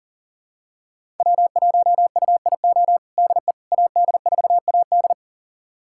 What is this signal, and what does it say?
Morse code, heard as a keyed tone, about 40 wpm, 700 Hz. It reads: W1UIO DE AD4AD
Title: USA Callsign DE Callsign - 5 Characters - 40wpm